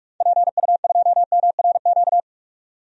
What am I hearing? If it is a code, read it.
Morse code, heard as a keyed tone, about 46 wpm, 700 Hz. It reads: PU2MRQ
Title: DXCC-100 Call Signs - 6 Characters - 45wpm